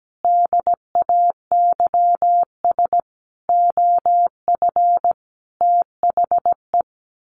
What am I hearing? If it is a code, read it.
Morse code, heard as a keyed tone, about 17 wpm, 700 Hz. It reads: DAYS OF THE
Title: Sets of 5 Words Encore 17wpm